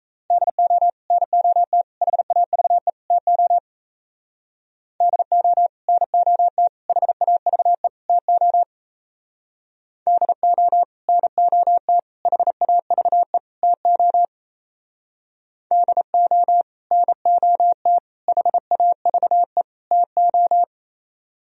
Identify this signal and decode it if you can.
Morse code, heard as a keyed tone, about 34 wpm, 700 Hz. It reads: DO NOT HAVE TO DO NOT HAVE TO DO NOT HAVE TO DO NOT HAVE TO
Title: Speed-Racing - Sets of 4 Words 28wpm